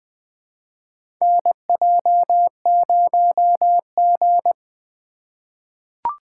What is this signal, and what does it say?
Morse code, heard as a keyed tone, about 20 wpm, 700 Hz. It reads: NJ0G
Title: USA Call Signs - 2x Character Spacing - 20wpm